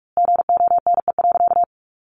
Morse code, heard as a keyed tone, about 45 wpm, 700 Hz. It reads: GONE.
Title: A Sound Of Thunder 45wpm